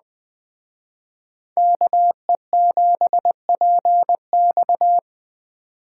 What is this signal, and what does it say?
Morse code, heard as a keyed tone, about 20 wpm, 700 Hz. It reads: KE7PX